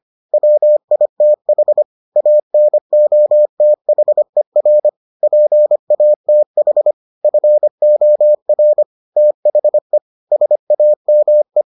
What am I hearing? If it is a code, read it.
Morse code, heard as a keyed tone, about 25 wpm, 600 Hz. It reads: WITH ANOTHER PATH FOR THE SAME